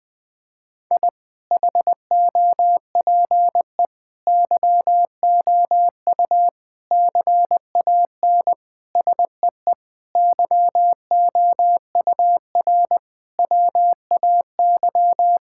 Morse code, heard as a keyed tone, about 20 wpm, 700 Hz. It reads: I HOPE YOU CAN SEE YOUR WAY
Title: Sentences with 8-Word Phrases - 20wpm